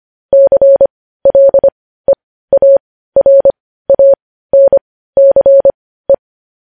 Morse code, heard as a keyed tone, about 25 wpm, spaced as for 15 wpm, 550 Hz. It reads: CLEARANCE
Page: news headlines in morse code at 15 wpm